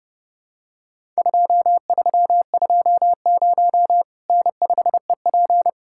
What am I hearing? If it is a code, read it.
Morse code, heard as a keyed tone, about 30 wpm, 700 Hz. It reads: 2320 N5EP
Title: Worked All Europe DX Contest - QTC Exchange - 30wpm